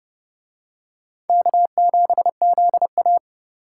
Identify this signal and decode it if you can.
Morse code, heard as a keyed tone, about 28 wpm, 700 Hz. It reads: K7ZA